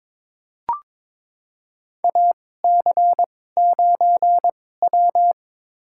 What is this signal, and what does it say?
Morse code, heard as a keyed tone, about 22 wpm, spaced as for 16 wpm, 700 Hz. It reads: AC9W
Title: USA Call Signs - 4 Characters - 2x Character Spacing - 22wpm